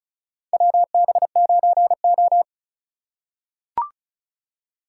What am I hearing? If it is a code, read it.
Morse code, heard as a keyed tone, about 34 wpm, 700 Hz. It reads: WB9O
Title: USA Call Signs - 4 Characters - 4x Character Spacing - 35wpm